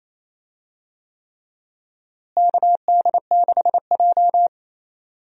Morse code, heard as a keyed tone, about 28 wpm, 700 Hz. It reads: KD6J
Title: USA Call Signs - 4 Characters - Repeated 3x - 28wpm